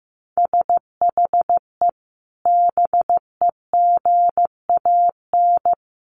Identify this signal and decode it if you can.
Morse code, heard as a keyed tone, about 15 wpm, 700 Hz. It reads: SHE BEGAN
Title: Sentences from Top 500 Words 15wpm